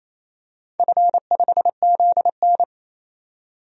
Morse code, heard as a keyed tone, about 28 wpm, 700 Hz. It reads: F5ZN
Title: DXCC-100 Call Signs - 4 Characters - 28wpm